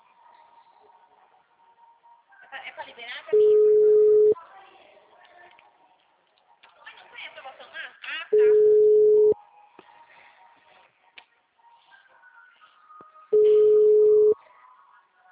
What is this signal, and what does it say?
Ringback tone (Europe, 425 Hz), with speech.